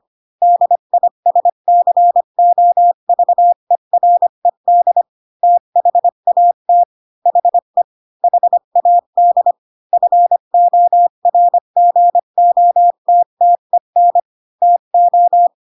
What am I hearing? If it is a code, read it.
Morse code, heard as a keyed tone, about 25 wpm, 700 Hz. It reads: DISCOVERED THAT HE HAD FORGOTTEN TO
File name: Pinocchio / Pinocchio ch3 20 25 ik1hge